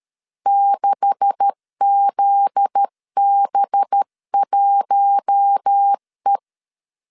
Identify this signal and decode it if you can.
Morse code, heard as a keyed tone, about 13 wpm, 800 Hz. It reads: 6ZB1E
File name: LEZIONI / 9 286 12wpm 2305